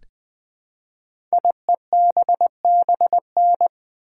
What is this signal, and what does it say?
Morse code, heard as a keyed tone, about 20 wpm, 700 Hz. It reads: IEBBN
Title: Code Groups - 5 Letters - 20wpm